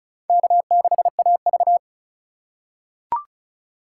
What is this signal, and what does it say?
Morse code, heard as a keyed tone, about 34 wpm, 700 Hz. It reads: K6AV
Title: USA Call Signs - 2x Character Spacing - 35wpm